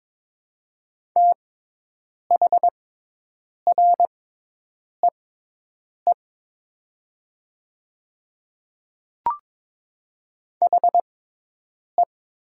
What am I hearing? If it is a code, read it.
Morse code, heard as a keyed tone, about 22 wpm, spaced as for 8 wpm, 700 Hz. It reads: THREE HE
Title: 6x Character Spacing - Top 200 Words - 22wpm